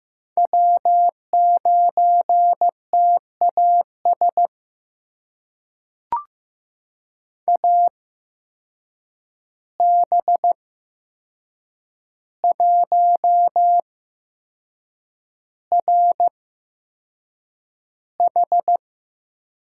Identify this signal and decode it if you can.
Morse code, heard as a keyed tone, about 15 wpm, spaced as for 6 wpm, 700 Hz. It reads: W9TAS AB1RH
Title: USA Call Signs - 5 Characters - 8x Character Spacing - 15wpm